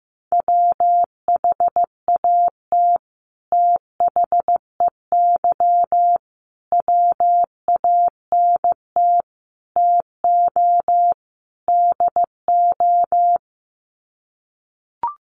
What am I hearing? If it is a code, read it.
Morse code, heard as a keyed tone, about 15 wpm, 700 Hz. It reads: WHAT THEY WANT TO DO
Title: Sets of 5 Words Encore2 15wpm